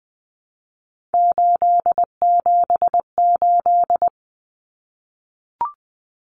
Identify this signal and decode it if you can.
Morse code, heard as a keyed tone, about 20 wpm, 700 Hz. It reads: 878
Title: Common Cut Numbers - 3-Digit 20wpm